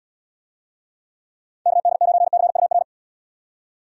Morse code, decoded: KD9CUR

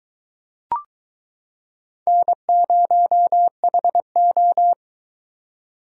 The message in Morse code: N0HO

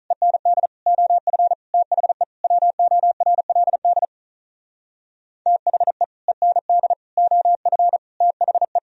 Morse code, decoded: END OF THE WORLD THE END OF THE

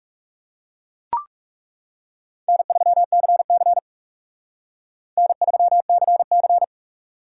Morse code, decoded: N3CC N3CC

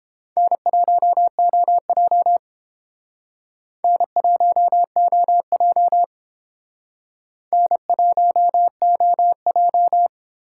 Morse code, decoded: N1OJ N1OJ N1OJ

